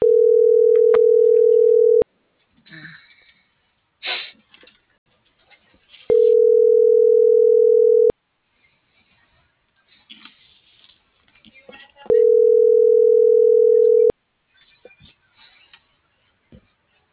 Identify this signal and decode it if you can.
Ringback tone (North America), 440 + 480 Hz.